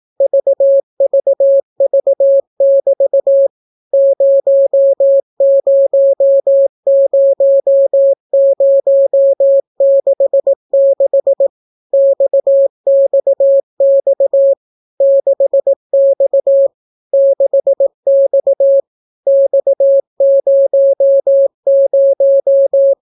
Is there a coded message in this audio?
Yes: Morse code from 0.2 to 22.9 s, heard as a keyed tone, about 18 wpm, 550 Hz. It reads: VVV<BT> 000066 XXX 6X 6X X00